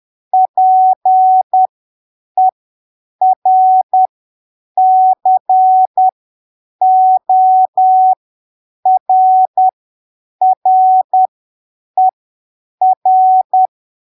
Morse code, heard as a keyed tone, about 10 wpm, spaced as for 7 wpm, 750 Hz. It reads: PERCORRER
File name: arquivos / texto / texto 19 10ppm